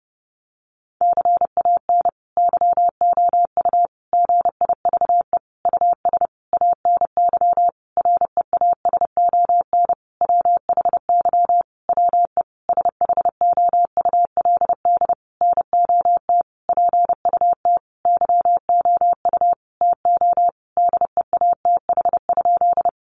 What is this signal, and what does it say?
Morse code, heard as a keyed tone, about 30 wpm, 700 Hz. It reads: CAN YOU GIVE US ANY REASON WHY WE SHOULD NOT PUT YOU TO DEATH?
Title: Sentences from Top 1100 Words 30wpm